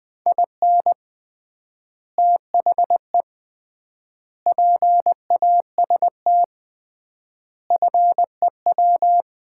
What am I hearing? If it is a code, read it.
Morse code, heard as a keyed tone, about 20 wpm, 700 Hz. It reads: IN THE PAST FEW
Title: 3x Word-Spacing - Sets of 5 Words - 20wpm